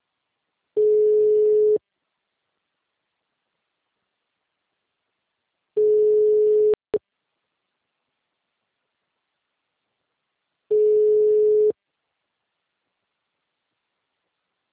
Ringback tone (Europe, 425 Hz).
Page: Campo Limpo - SP